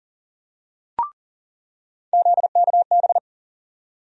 Morse code, heard as a keyed tone, about 40 wpm, 700 Hz. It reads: ZKB
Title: USA Call Sign Postfixes - 3 Letters 40wpm